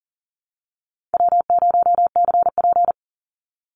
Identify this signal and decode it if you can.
Morse code, heard as a keyed tone, about 40 wpm, 700 Hz. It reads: W0CP
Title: NA SOTA Activator Callsigns 40wpm